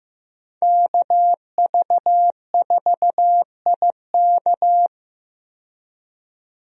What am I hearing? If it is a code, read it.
Morse code, heard as a keyed tone, about 15 wpm, 700 Hz. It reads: KV4IK
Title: USA Call Signs - 5 Characters - 15wpm